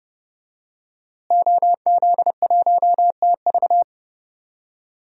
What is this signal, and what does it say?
Morse code, heard as a keyed tone, about 30 wpm, 700 Hz. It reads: OZ1TV
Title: DXCC-100 Call Signs - 5 Characters - Repeated 3x - 30wpm